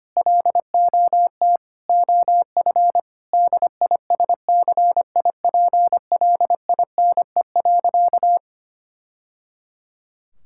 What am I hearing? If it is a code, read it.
Morse code, heard as a keyed tone, about 25 wpm, 700 Hz. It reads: LOT OF DISCIPLINE.
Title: Sentences from Top 1000 Words - Review Part 6 - 25wpm